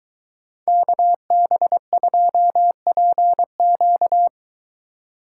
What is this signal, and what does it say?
Morse code, heard as a keyed tone, about 22 wpm, 700 Hz. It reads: KB2PQ